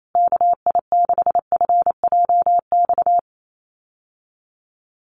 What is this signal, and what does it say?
Morse code, heard as a keyed tone, about 28 wpm, 700 Hz. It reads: KI6FJX
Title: USA Call Signs Repeated 3x 28wpm